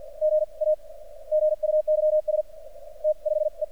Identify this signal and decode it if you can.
Morse code, heard as a keyed tone, about 46 wpm, spaced as for 41 wpm, 610 Hz. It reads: GA GUYS T 5 E
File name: archived / CRSnet